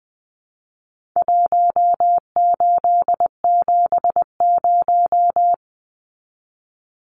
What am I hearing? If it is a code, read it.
Morse code, heard as a keyed tone, about 20 wpm, 700 Hz. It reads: 1870
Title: Numbers - Contest Style - 4-Digit 20wpm